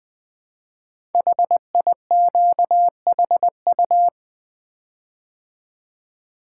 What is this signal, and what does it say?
Morse code, heard as a keyed tone, about 20 wpm, 700 Hz. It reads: HIQHU
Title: Code Groups - 5 Letters - 20wpm